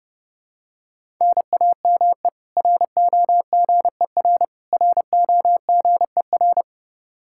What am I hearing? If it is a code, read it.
Morse code, heard as a keyed tone, about 30 wpm, 700 Hz. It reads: NAME ROGER ROGER
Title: Practice QSO - Exchange 1 - Beginning - 30wpm